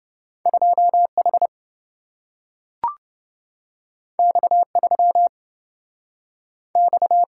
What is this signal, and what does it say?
Morse code, heard as a keyed tone, about 27 wpm, 700 Hz. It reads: 2H X3 X